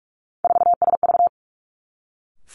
Morse code, heard as a keyed tone, about 46 wpm, 700 Hz. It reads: 4SV
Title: Three Challenging Letter-Number Combinations 45wpm